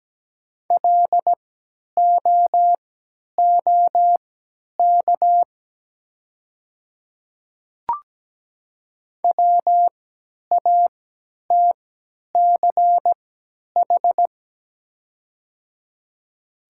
Morse code, heard as a keyed tone, about 17 wpm, spaced as for 10 wpm, 690 Hz. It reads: LOOK WATCH
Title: 3x Character Spacing - Top 300 Words - 17wpm